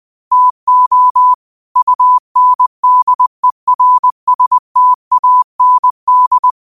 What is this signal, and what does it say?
Morse code, heard as a keyed tone, about 20 wpm, 1000 Hz. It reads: TO UNDERSTAND
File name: qotdfiles / 20wpm